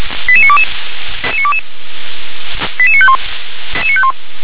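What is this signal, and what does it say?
Five-tone selective call (ZVEI), heard as quick runs of five tones, tones of about 70 ms. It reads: 88922 88951
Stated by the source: ZVEI five-tone selective call received on an SDR.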